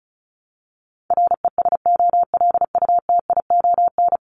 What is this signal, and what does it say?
Morse code, heard as a keyed tone, about 34 wpm, 700 Hz. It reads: RESOLUTION
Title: Top 1501-1750 Words 35wpm